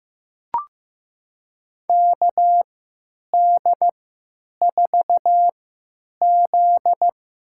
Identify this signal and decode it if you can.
Morse code, heard as a keyed tone, about 15 wpm, spaced as for 9 wpm, 700 Hz. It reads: KD4Z
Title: USA Call Signs - 6 Characters - 3x Character Spacing - 15wpm